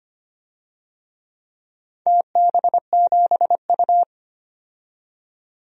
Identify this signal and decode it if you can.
Morse code, heard as a keyed tone, about 25 wpm, 700 Hz. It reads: TB7U
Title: DXCC-100 Call Signs - 4 Characters - 25wpm